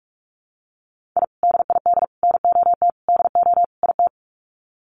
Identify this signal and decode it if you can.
Morse code, heard as a keyed tone, about 44 wpm, 700 Hz. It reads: I DID NOT DO IT
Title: Sets of 5 Words Encore2 45wpm